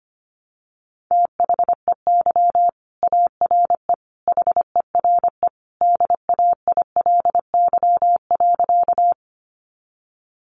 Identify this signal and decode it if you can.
Morse code, heard as a keyed tone, about 25 wpm, 700 Hz. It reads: THEY ARE HERE DAILY.
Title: Sentences from Top 1100 Words 25wpm